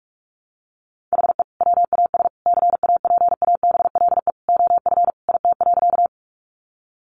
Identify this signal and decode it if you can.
Morse code, heard as a keyed tone, about 44 wpm, 700 Hz. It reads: HE WAS CAPABLE OF IT.
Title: Sentences from Top 1200 Words 45wpm